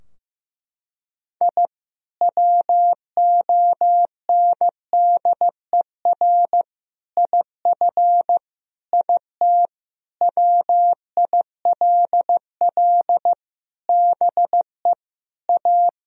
Morse code, heard as a keyed tone, about 15 wpm, 700 Hz. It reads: I WONDER IF IT WILL BE A
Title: Sentences from Top 1000 Words - Review Part 3 - 15wpm